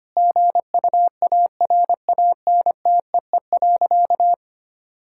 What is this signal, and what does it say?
Morse code, heard as a keyed tone, about 25 wpm, 700 Hz. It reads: GUARANTEE.
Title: Sentences from Top 1600 Words 25wpm